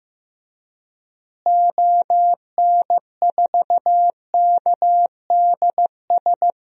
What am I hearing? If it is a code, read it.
Morse code, heard as a keyed tone, about 15 wpm, 700 Hz. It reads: ON4KDS
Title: DXCC-100 Call Signs - 6 Characters - Repeated 3x - 15wpm